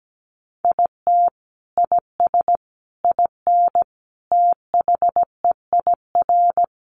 Morse code, heard as a keyed tone, about 17 wpm, 700 Hz. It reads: IT IS IN THEIR
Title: Sentences from Top 1300 Words 17wpm